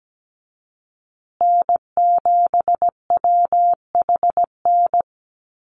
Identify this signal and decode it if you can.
Morse code, heard as a keyed tone, about 17 wpm, 700 Hz. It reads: N7WHN
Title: USA Call Signs Repeated 2x 17wpm